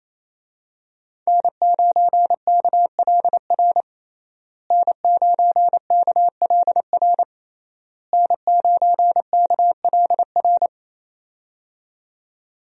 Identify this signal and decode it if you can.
Morse code, heard as a keyed tone, about 28 wpm, 700 Hz. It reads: N9KLR N9KLR N9KLR
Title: USA Call Signs - 5 Characters - Repeated 3x - 28wpm